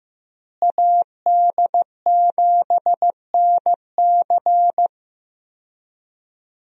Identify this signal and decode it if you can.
Morse code, heard as a keyed tone, about 15 wpm, 700 Hz. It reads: AD7NC
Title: USA Call Signs - 5 Characters - Repeated 3x - 15wpm